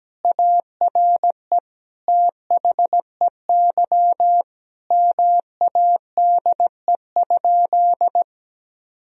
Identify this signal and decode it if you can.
Morse code, heard as a keyed tone, about 17 wpm, 700 Hz. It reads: ARE THEY MADE?
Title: Sentences from Top 1000 Words - Review Part 3 - 17wpm